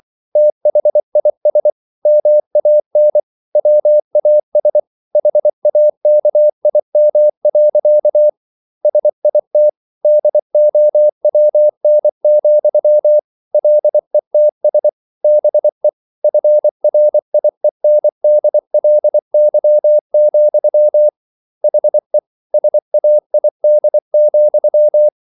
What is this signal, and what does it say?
Morse code, heard as a keyed tone, about 24 wpm, 600 Hz. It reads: THIS MAN WAS HAKIM. SIT DOWN, LETS BE FRIENDLY, HE SAID,